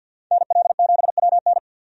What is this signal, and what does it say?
Morse code, heard as a keyed tone, about 53 wpm, 700 Hz. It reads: NL7WN
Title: USA Call Signs - 5 Characters - 50wpm